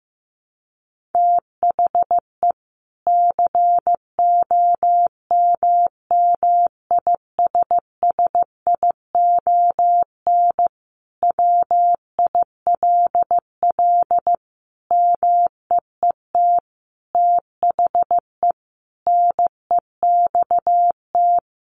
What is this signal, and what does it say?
Morse code, heard as a keyed tone, about 15 wpm, 700 Hz. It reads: THE COMMISSION WILL MEET THE NEXT
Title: Sentences from Top 1400 Words 15wpm